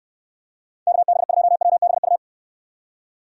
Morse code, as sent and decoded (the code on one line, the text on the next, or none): -.- -... ..--- .-- -... ..-
KB2WBU